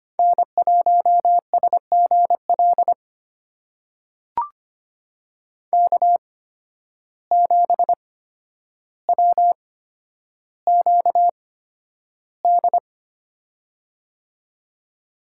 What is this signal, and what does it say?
Morse code, heard as a keyed tone, about 25 wpm, spaced as for 11 wpm, 700 Hz. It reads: N1SGL K7WQD
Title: USA Call Signs - 5 Characters - 8x Character Spacing - 25wpm